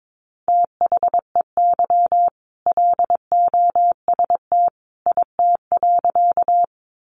Morse code, heard as a keyed tone, about 22 wpm, 700 Hz. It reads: THEY LOST IT.
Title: Sentences from Top 1400 Words 22wpm